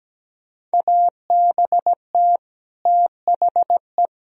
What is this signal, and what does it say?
Morse code, heard as a keyed tone, about 17 wpm, 700 Hz. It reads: ABT THE
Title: Sentences from Top 400 Words and Abbreviations 17wpm